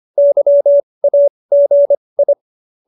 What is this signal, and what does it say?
Morse code, heard as a keyed tone, about 25 wpm, spaced as for 20 wpm, 570 Hz. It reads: YAGI